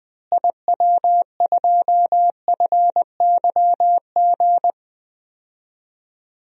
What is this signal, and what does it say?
Morse code, heard as a keyed tone, about 20 wpm, 700 Hz. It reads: IW2FYG
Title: DXCC-100 Call Signs - 6 Characters - Repeated 3x - 20wpm